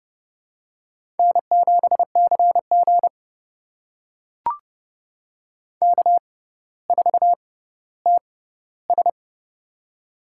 Morse code, heard as a keyed tone, about 32 wpm, spaced as for 16 wpm, 700 Hz. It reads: N7CG K4TS
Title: USA Call Signs - 4 Characters - 6x Character Spacing - 30wpm